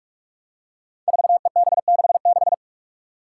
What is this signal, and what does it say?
Morse code, heard as a keyed tone, about 41 wpm, 700 Hz. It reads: 4EB66
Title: Five Challenging Letter-Number Combinations 45wpm